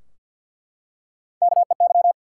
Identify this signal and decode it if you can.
Morse code, heard as a keyed tone, about 49 wpm, 700 Hz. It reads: KEY
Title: Three Letter Words - 50wpm